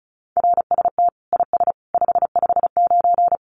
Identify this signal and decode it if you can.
Morse code, heard as a keyed tone, about 34 wpm, 700 Hz. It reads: RST IS 559
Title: Common QSO Phrases 35wpm